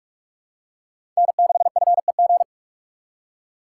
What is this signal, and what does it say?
Morse code, heard as a keyed tone, about 44 wpm, 700 Hz. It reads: N6FEG
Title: USA Call Signs - 5 Characters - 3x Character Spacing - 45wpm